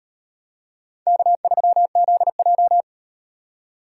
Morse code, heard as a keyed tone, about 36 wpm, 700 Hz. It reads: K3ZJ